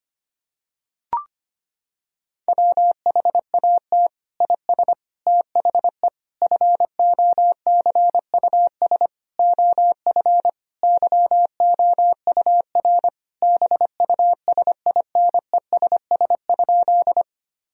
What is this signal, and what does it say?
Morse code, heard as a keyed tone, about 25 wpm, 700 Hz. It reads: WHAT IS THE FOCUS OF YOUR BUSINESS?
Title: Sentences from Top 1000 Words - Review Part 4 - 25wpm